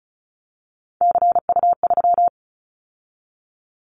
Morse code, heard as a keyed tone, about 36 wpm, 700 Hz. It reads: CU3
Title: Callsign Prefixes for Top 50 DXCC - 35wpm